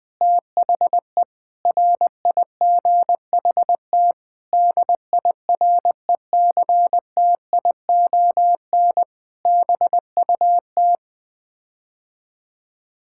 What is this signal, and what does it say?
Morse code, heard as a keyed tone, about 20 wpm, 700 Hz. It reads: THE RIGHT DIRECTION BUT
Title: Sets of 8 Words - 20wpm